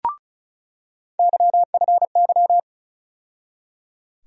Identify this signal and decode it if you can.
Morse code, heard as a keyed tone, about 34 wpm, 700 Hz. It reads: YFY
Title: USA Call Sign Postfixes - 3 Letters 35wpm